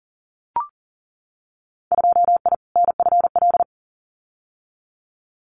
Morse code, heard as a keyed tone, about 40 wpm, 700 Hz. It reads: 2I NFL
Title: Winter Field Day Exchanges 40wpm